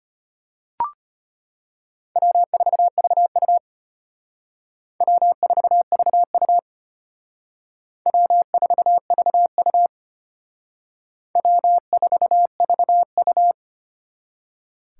Morse code, heard as a keyed tone, about 32 wpm, 700 Hz. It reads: W4VU W4VU W4VU W4VU